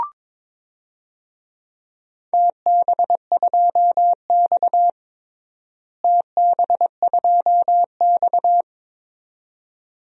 Morse code, heard as a keyed tone, about 21 wpm, 700 Hz. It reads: TB2X TB2X
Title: DXCC-100 Call Signs - 4 Characters - Repeated 2x - 22wpm